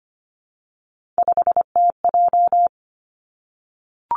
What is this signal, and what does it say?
Morse code, heard as a keyed tone, about 25 wpm, 700 Hz. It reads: 5TJ
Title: Three Challenging Letter-Number Combinations 25wpm